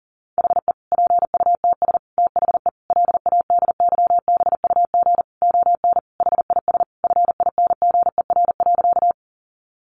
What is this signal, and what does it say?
Morse code, heard as a keyed tone, about 37 wpm, 700 Hz. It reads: HE PUTS THE LADYBUG ON HIS FINGER.
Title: Additional Easy Phrases 40wpm